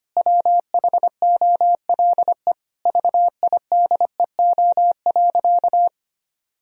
Morse code, heard as a keyed tone, about 25 wpm, 700 Hz. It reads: WHOLE VIDEO.
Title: Sentences from Top 1000 Words - Review Part 4 - 25wpm